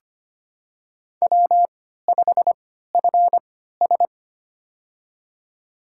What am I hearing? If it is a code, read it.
Morse code, heard as a keyed tone, about 25 wpm, spaced as for 14 wpm, 700 Hz. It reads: W5FS
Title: USA Call Signs - 4 Characters - 3x Character Spacing - 25wpm